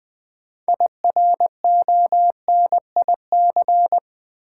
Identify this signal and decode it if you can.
Morse code, heard as a keyed tone, about 20 wpm, 700 Hz. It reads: IRONIC
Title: Build A Word - 2 Letters At A Time - 6-Letter Words - 20wpm